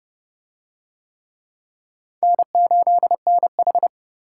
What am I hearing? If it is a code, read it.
Morse code, heard as a keyed tone, about 28 wpm, 700 Hz. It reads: N8NH